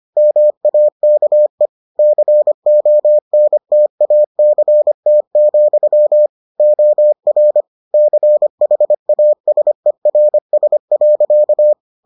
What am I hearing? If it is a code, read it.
Morse code, heard as a keyed tone, about 25 wpm, 600 Hz. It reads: MAKE CONTACT, OR CHASERS.